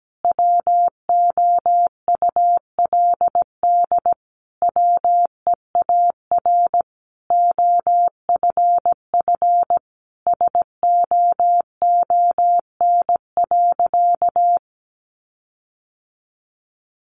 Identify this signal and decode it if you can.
Morse code, heard as a keyed tone, about 17 wpm, 700 Hz. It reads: WOULD WEAR OFF SOON.
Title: Sentences from Top 1500 Words 17wpm